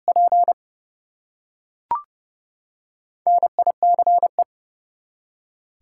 Morse code, heard as a keyed tone, about 30 wpm, 700 Hz. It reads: P NICE
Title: Four Letter Words - 30wpm